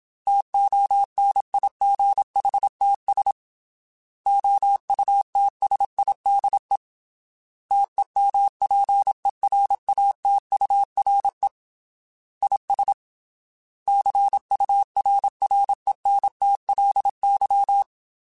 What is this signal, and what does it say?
Morse code, heard as a keyed tone, about 27 wpm, 800 Hz. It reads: TONIGHTS OUTSIDE TEMPERATURE IS CURRENTLY